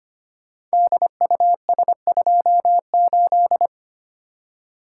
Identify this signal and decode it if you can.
Morse code, heard as a keyed tone, about 25 wpm, 690 Hz. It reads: DUS28
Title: Code Groups - 5 Letter:Numbers - 25wpm